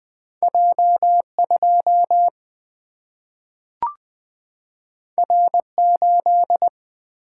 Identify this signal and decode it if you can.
Morse code, heard as a keyed tone, about 20 wpm, 700 Hz. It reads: J 2 R 8
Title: Single-Letter Single-Number 20wpm